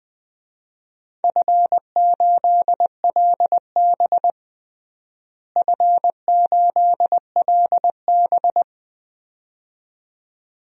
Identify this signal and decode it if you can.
Morse code, heard as a keyed tone, about 20 wpm, 700 Hz. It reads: F8LB F8LB
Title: DXCC-100 Call Signs - 4 Characters - Repeated 2x - 20wpm